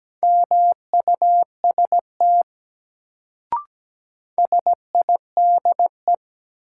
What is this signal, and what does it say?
Morse code, heard as a keyed tone, about 17 wpm, 700 Hz. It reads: MUST SIDE
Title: Sets of 2 Words - New Word Drill - 17wpm